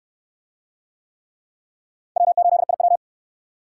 Morse code, heard as a keyed tone, about 62 wpm, 700 Hz. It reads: W8IW